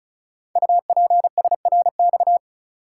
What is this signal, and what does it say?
Morse code, heard as a keyed tone, about 36 wpm, 700 Hz. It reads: UPSRX
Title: Code Groups - 5 Letters - 35wpm